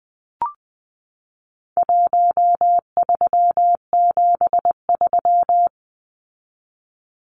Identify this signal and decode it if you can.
Morse code, heard as a keyed tone, about 20 wpm, 700 Hz. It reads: 1373
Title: Numbers - Contest Style - 4-Digit 20wpm